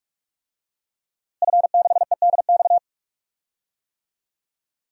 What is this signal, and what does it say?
Morse code, heard as a keyed tone, about 42 wpm, 700 Hz. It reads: F6EDX
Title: DXCC-100 Call Signs - 5 Characters - Repeated 3x - 45wpm